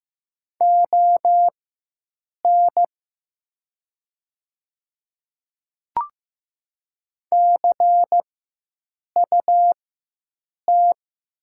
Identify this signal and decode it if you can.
Morse code, heard as a keyed tone, about 15 wpm, spaced as for 7 wpm, 700 Hz. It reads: ON CUT
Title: 4x Character Spacing - Top 500 Words - 15wpm